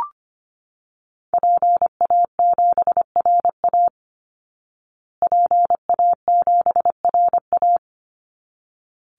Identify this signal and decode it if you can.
Morse code, heard as a keyed tone, about 25 wpm, 700 Hz. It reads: PA7RA PA7RA
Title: Call Signs Repeated 3x - 25wpm